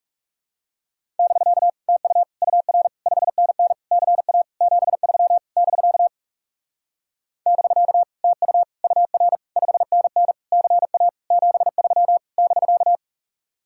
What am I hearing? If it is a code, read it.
Morse code, heard as a keyed tone, about 39 wpm, 700 Hz. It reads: <BK> TU UR 5NN CA 73 <BK> <BK> TU UR 5NN CA 73 <BK>